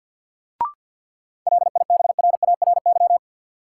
Morse code, heard as a keyed tone, about 53 wpm, 700 Hz. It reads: LIBRARY
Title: Sentences from Top 600 Words - New Word Drill - 50wpm